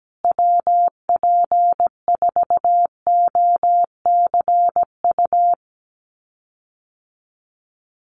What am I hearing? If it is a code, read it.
Morse code, heard as a keyed tone, about 17 wpm, 700 Hz. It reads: WP4OCU
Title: USA Call Signs Repeated 3x 17wpm